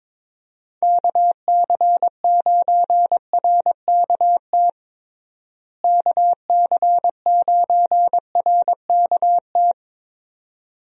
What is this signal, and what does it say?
Morse code, heard as a keyed tone, about 22 wpm, 700 Hz. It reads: KC9RKT KC9RKT
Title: USA Call Signs - 6 Characters - Repeated 2x - 22wpm